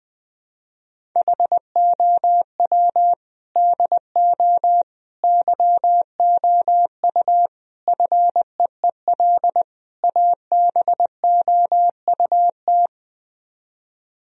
Morse code, heard as a keyed tone, about 21 wpm, 700 Hz. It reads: HOW DO YOU FEEL ABOUT